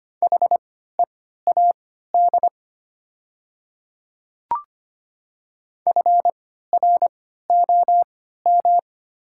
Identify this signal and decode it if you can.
Morse code, heard as a keyed tone, about 25 wpm, spaced as for 14 wpm, 700 Hz. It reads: HEAD FROM
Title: 3x Character Spacing - Top 300 Words - 25wpm